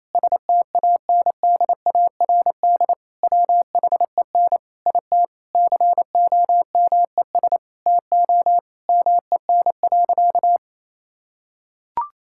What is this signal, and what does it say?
Morse code, heard as a keyed tone, about 27 wpm, 700 Hz. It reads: STANDARD WHEN IT COMES TO MEN.
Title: Sentences from Top 1000 Words - Review Part 6 - 28wpm